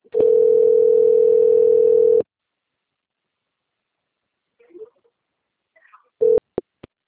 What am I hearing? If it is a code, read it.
Ringback tone (North America), 440 + 480 Hz.